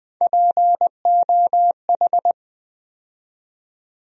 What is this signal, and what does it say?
Morse code, heard as a keyed tone, about 20 wpm, 700 Hz. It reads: POH
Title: USA Call Sign Postfixes - 3 Letters 20wpm